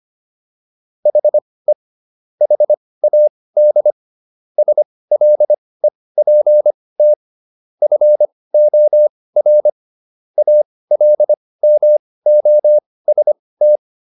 Morse code, heard as a keyed tone, about 25 wpm, spaced as for 18 wpm, 600 Hz. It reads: HE HAD SLEPT FOR ALMOST